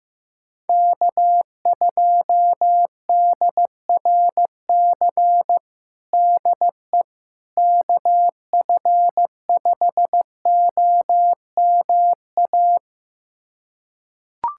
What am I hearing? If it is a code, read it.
Morse code, heard as a keyed tone, about 15 wpm, 700 Hz. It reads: K2DRC DE KF5OMA
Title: USA Callsign DE Callsign 15wpm